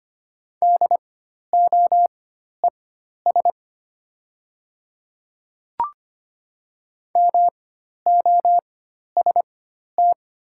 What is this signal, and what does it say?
Morse code, heard as a keyed tone, about 25 wpm, spaced as for 12 wpm, 700 Hz. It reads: DOES MOST
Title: 4x Character Spacing - Top 300 Words - 25wpm